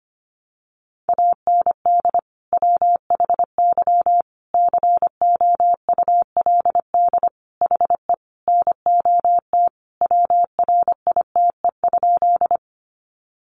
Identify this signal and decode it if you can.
Morse code, heard as a keyed tone, about 25 wpm, 700 Hz. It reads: AND WHY COULD HE NOT WRITE?
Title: Sentences from Top 600 Words 25wpm